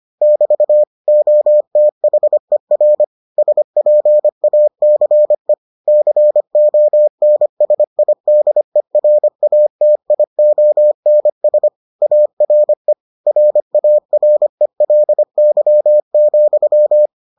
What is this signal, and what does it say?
Morse code, heard as a keyed tone, about 25 wpm, 600 Hz. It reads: <BT> OTHER SPACE CONSIDERATIONS ARE RARELY,